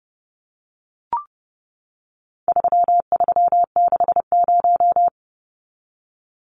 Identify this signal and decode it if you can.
Morse code, heard as a keyed tone, about 30 wpm, 700 Hz. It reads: 3360
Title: Numbers - 4-Digit 30wpm